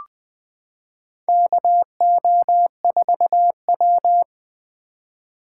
Morse code, heard as a keyed tone, about 20 wpm, 700 Hz. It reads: KO4W
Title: Four Letter-Number Combinations - 20wpm